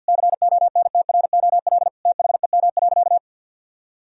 Morse code, heard as a keyed tone, about 51 wpm, 700 Hz. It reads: CONTROL THEM.
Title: Sentences from Top 1000 Words - Review Part 2 - 50wpm